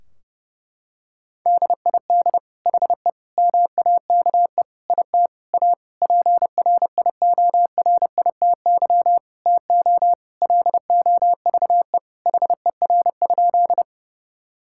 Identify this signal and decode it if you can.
Morse code, heard as a keyed tone, about 32 wpm, 700 Hz. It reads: DID HE MAKE IT A PRIORITY TO LOVE HER?
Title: Sentences from Top 1600 Words 30wpm